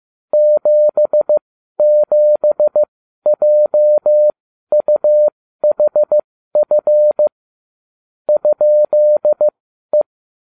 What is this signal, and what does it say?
Morse code, heard as a keyed tone, about 15 wpm, spaced as for 12 wpm, 600 Hz. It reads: 77JUHF ?E